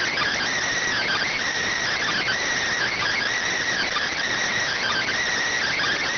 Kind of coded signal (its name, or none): Emergency Alert System (SAME) tones